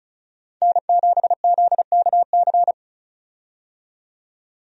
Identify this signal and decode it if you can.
Morse code, heard as a keyed tone, about 34 wpm, 700 Hz. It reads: N7ZKC
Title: USA Call Signs - 5 Characters - 35wpm